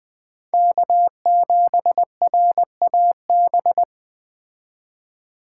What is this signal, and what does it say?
Morse code, heard as a keyed tone, about 20 wpm, 700 Hz. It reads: K7RAB
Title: Speed Racing - USA 1x3 Call Signs 17wpm